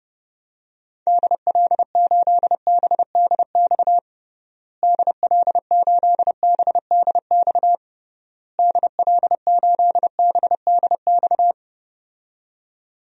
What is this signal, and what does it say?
Morse code, heard as a keyed tone, about 30 wpm, 700 Hz. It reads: DL8BDX DL8BDX DL8BDX
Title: DXCC-100 Call Signs - 6 Characters - Repeated 3x - 30wpm